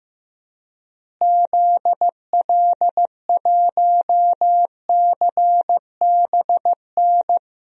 Morse code, heard as a keyed tone, about 15 wpm, 700 Hz. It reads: ZL1CBN
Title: DXCC-100 Call Signs - 6 Characters - Repeated 3x - 15wpm